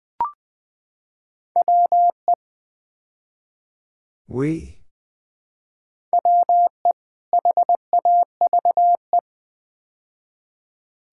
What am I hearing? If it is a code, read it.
Morse code, heard as a keyed tone, about 20 wpm, 700 Hz. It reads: WE WE HAVE
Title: Build A Phrase - 5-Word Phrases - 20wpm